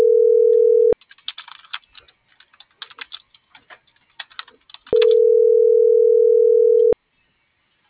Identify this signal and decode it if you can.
Ringback tone (North America), 440 + 480 Hz.